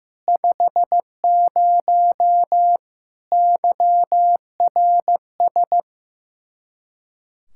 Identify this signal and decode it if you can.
Morse code, heard as a keyed tone, about 15 wpm, 700 Hz. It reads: 50 YRS
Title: K1USN SST QSO Practice 15wpm